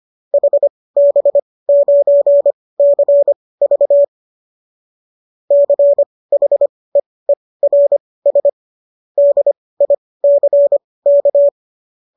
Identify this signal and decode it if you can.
Morse code, heard as a keyed tone, about 25 wpm, spaced as for 18 wpm, 570 Hz. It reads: HB9CV CHEERS DICK